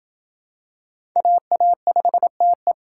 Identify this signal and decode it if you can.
Morse code, heard as a keyed tone, about 27 wpm, 700 Hz. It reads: AA5TE